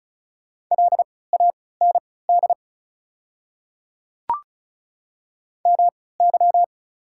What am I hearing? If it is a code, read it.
Morse code, heard as a keyed tone, about 34 wpm, spaced as for 20 wpm, 700 Hz. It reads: LAND MY
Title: 3x Character Spacing - Top 300 Words - 35wpm